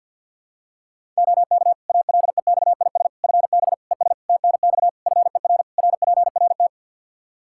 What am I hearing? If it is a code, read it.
Morse code, heard as a keyed tone, about 53 wpm, 700 Hz. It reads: OK ALEXIS FB ES TNX FER RPRT
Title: Practice QSO - Exchange 2 - Info - 50wpm